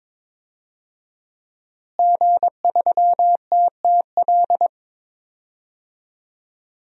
Morse code, heard as a keyed tone, about 22 wpm, 700 Hz. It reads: G3TTL
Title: DXCC-100 Call Signs - 5 Characters - Repeated 3x - 22wpm